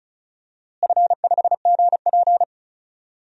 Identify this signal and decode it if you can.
Morse code, heard as a keyed tone, about 36 wpm, 700 Hz. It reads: F5GP
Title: DXCC-100 Call Signs - 4 Characters - Repeated 2x - 35wpm